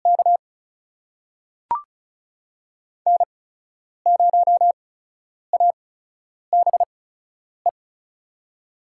Morse code, heard as a keyed tone, about 36 wpm, spaced as for 10 wpm, 700 Hz. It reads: K N0ABE
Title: USA Call Signs - 5 Characters - 8x Character Spacing - 35wpm